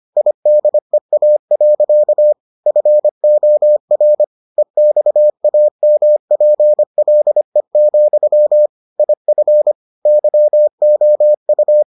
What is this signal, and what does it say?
Morse code, heard as a keyed tone, about 25 wpm, 600 Hz. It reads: IDEA. FOR EXAMPLE, IF YOU